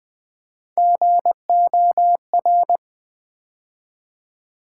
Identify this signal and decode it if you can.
Morse code, heard as a keyed tone, about 20 wpm, 700 Hz. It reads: GOR
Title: Build A Word - 3 Letters At A Time - 8-Letter Words - 20wpm